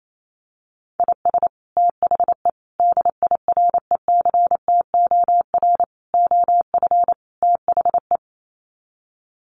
Morse code, heard as a keyed tone, about 28 wpm, 700 Hz. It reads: IS THE DIRECTOR OF THE
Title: Sets of 5 Words Encore2 28wpm